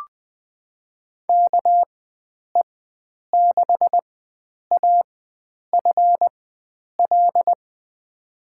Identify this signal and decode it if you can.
Morse code, heard as a keyed tone, about 20 wpm, spaced as for 9 wpm, 700 Hz. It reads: KE6AFL
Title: USA Call Signs - 6 Characters - 4x Character Spacing - 20wpm